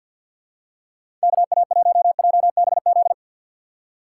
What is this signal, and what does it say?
Morse code, heard as a keyed tone, about 51 wpm, 700 Hz. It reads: KA1JBZ
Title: USA Call Signs - 4x Character Spacing - 50wpm